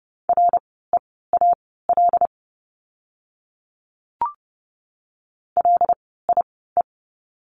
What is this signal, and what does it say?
Morse code, heard as a keyed tone, about 30 wpm, spaced as for 17 wpm, 700 Hz. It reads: REAL LIE
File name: Lesson 16f - Letter c - Words 30wpm